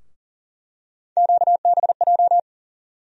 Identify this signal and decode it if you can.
Morse code, heard as a keyed tone, about 40 wpm, 700 Hz. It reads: QBJ
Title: USA Call Sign Postfixes - 3 Letters 40wpm